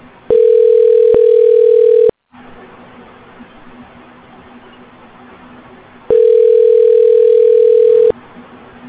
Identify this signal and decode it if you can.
Ringback tone (North America), 440 + 480 Hz.